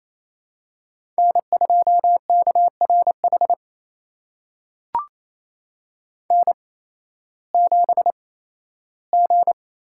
Morse code, heard as a keyed tone, about 28 wpm, spaced as for 15 wpm, 700 Hz. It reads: N2KRH N7G
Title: USA Call Signs - 5 Characters - 8x Character Spacing - 28wpm